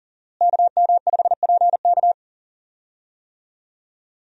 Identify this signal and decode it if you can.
Morse code, heard as a keyed tone, about 40 wpm, 700 Hz. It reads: KM5PK
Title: USA Call Signs - 5 Characters - Repeated 3x - 40wpm